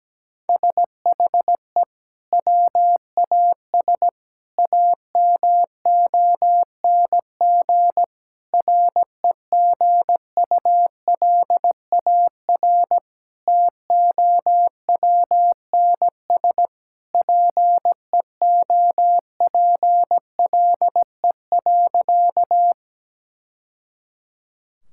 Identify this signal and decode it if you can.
Morse code, heard as a keyed tone, about 17 wpm, 700 Hz. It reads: SHE WAS AMONG REGULAR TOWNS PEOPLE.
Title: Sentences from Top 1000 Words - Review Part 6 - 17wpm